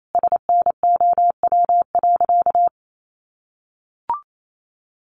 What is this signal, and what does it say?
Morse code, heard as a keyed tone, about 28 wpm, 700 Hz. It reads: SNOW.
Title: Sentences from Top 1500 Words 28wpm